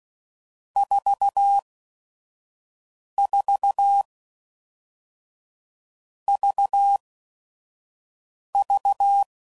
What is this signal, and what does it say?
Morse code, heard as a keyed tone, about 16 wpm, spaced as for 5 wpm, 800 Hz. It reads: 44 VV
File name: Groups / Basic Letter Groups / VB46